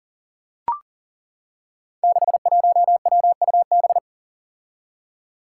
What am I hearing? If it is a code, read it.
Morse code, heard as a keyed tone, about 40 wpm, 700 Hz. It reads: 61WUB
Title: Five Challenging Letter-Number Combinations 40wpm